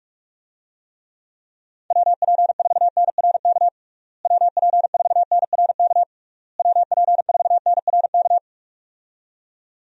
Morse code, heard as a keyed tone, about 41 wpm, 700 Hz. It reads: WP4NRK WP4NRK WP4NRK
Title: USA Call Signs - 6 Characters - Repeated 3x - 45wpm